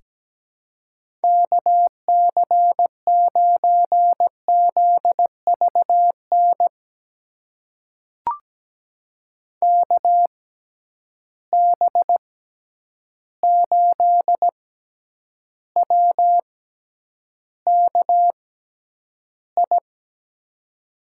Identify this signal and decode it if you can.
Morse code, heard as a keyed tone, about 17 wpm, spaced as for 9 wpm, 700 Hz. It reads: KC9ZVN KB8WKI
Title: USA Call Signs - 6 Characters - 6x Character Spacing - 17wpm